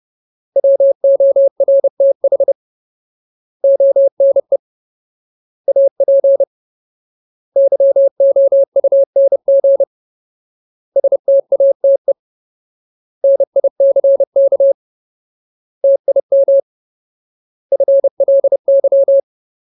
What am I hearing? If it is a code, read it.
Morse code, heard as a keyed tone, about 30 wpm, 550 Hz. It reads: WORTH ONE AP YOUNG STATE NICK TIM FLY
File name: Ultimate Word List / 30WPM Ultimate